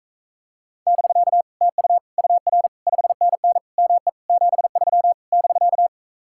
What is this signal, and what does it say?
Morse code, heard as a keyed tone, about 36 wpm, 700 Hz. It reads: <BK> TU UR 5NN ME 73 <BK>